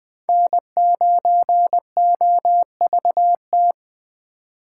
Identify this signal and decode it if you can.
Morse code, heard as a keyed tone, about 21 wpm, 700 Hz. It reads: N9OVT
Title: USA Call Signs - 5 Characters - 4x Character Spacing - 20wpm